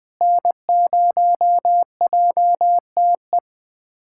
Morse code, heard as a keyed tone, about 20 wpm, 700 Hz. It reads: N0JTE
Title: USA Callsign DE Callsign - 5 Characters - 20wpm